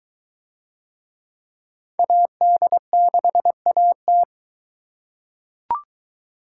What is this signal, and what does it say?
Morse code, heard as a keyed tone, about 22 wpm, 700 Hz. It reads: AD6AT